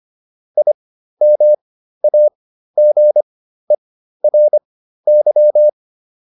Morse code, heard as a keyed tone, about 25 wpm, spaced as for 13 wpm, 600 Hz. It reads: IMAGERY
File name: IM7 13